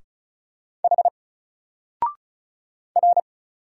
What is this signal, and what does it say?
Morse code, heard as a keyed tone, about 34 wpm, 700 Hz. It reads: H R
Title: Single Character (Letter:Number:Punctuation:Prosign) Alt 35wpm